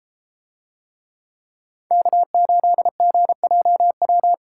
Morse code, heard as a keyed tone, about 32 wpm, 700 Hz. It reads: K8GJW